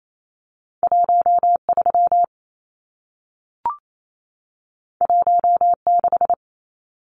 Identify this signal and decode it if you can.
Morse code, heard as a keyed tone, about 28 wpm, 700 Hz. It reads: 13 16
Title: Numbers - 2-Digit 28wpm